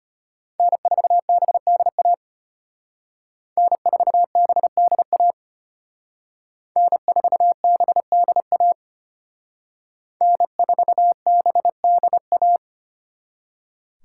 Morse code, heard as a keyed tone, about 32 wpm, 700 Hz. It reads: N4BDA N4BDA N4BDA N4BDA